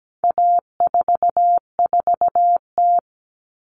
Morse code, heard as a keyed tone, about 17 wpm, 700 Hz. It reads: A44T
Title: All Cut Numbers - 4-Digit 17wpm